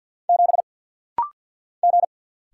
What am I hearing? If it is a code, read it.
Morse code, heard as a keyed tone, about 50 wpm, 700 Hz. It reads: 7 G
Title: Single Letter-Number Alt - Rapid-Fire 50wpm